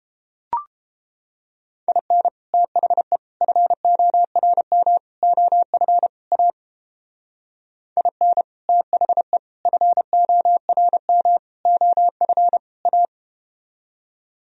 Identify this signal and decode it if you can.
Morse code, heard as a keyed tone, about 30 wpm, 700 Hz. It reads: IN THE FORM OF A IN THE FORM OF A